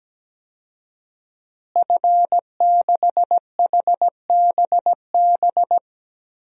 Morse code, heard as a keyed tone, about 17 wpm, 700 Hz. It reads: F6HBB
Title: DXCC-100 Call Signs - 5 Characters - Repeated 2x - 17wpm